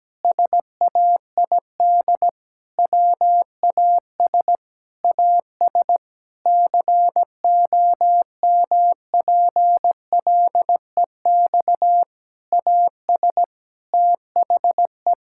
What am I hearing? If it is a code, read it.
Morse code, heard as a keyed tone, about 17 wpm, 700 Hz. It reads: SAID WAS AS COMPLEX AS THE
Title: Sentences from Top 1000 Words - Review Part 6 - 17wpm